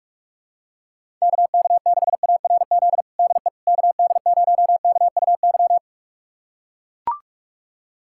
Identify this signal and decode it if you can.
Morse code, heard as a keyed tone, about 41 wpm, 700 Hz. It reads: KK6ARZ DE KD0KUY
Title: USA Callsign DE Callsign 45wpm